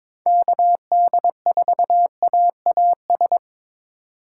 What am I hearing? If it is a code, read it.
Morse code, heard as a keyed tone, about 22 wpm, 700 Hz. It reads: KD4AAS
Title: USA Call Signs - 6 Characters - 2x Character Spacing - 22wpm